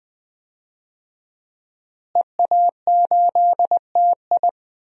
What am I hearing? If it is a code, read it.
Morse code, heard as a keyed tone, about 20 wpm, 700 Hz. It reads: EA8TI
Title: DXCC-100 Call Signs - 5 Characters - Repeated 2x - 20wpm